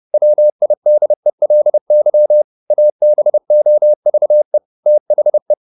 Morse code, heard as a keyed tone, about 30 wpm, 600 Hz. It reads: WIDELY ABOVE THE